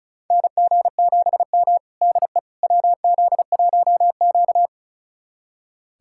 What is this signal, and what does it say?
Morse code, heard as a keyed tone, about 36 wpm, 700 Hz. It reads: NG7M DE WZ1Q
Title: USA Callsign DE Callsign - 4 Characters - 35wpm